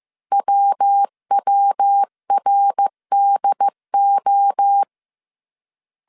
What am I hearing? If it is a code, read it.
Morse code, heard as a keyed tone, about 15 wpm, 800 Hz. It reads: WWRDO